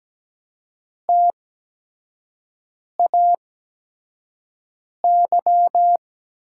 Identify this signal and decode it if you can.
Morse code, heard as a keyed tone, about 17 wpm, spaced as for 5 wpm, 700 Hz. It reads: TAY
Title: 8x Character Spacing - Top 500 Words - 17wpm